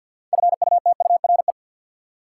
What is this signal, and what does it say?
Morse code, heard as a keyed tone, about 50 wpm, 700 Hz. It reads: FUTURE
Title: Sets of 3 Words - New Word Drill - 50wpm